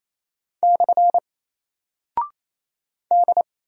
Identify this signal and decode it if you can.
Morse code, heard as a keyed tone, about 27 wpm, 700 Hz. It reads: / D
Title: Single Character (Letter:Number:Punctuation:Prosign) 28wpm